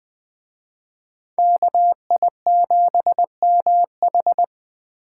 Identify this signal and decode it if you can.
Morse code, heard as a keyed tone, about 24 wpm, 700 Hz. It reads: KI7MH
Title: Speed Racing - USA 2x2 Call Signs 20wpm